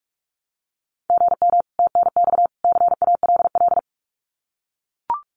Morse code, heard as a keyed tone, about 44 wpm, 700 Hz. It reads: GM TNX CALL
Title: Common QSO Phrases 45wpm